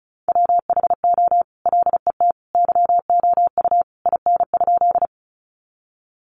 Morse code, heard as a keyed tone, about 34 wpm, 700 Hz. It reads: WHO LET YOU IN?
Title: Sentences from Top 300 Words 35wpm